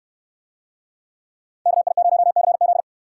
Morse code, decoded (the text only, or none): KI0CZ